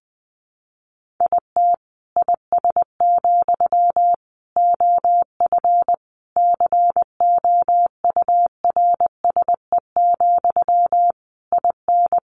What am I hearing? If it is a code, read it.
Morse code, heard as a keyed tone, about 20 wpm, 700 Hz. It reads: IT IS, OF COURSE, IN
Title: Sentences from Top 1200 Words 20wpm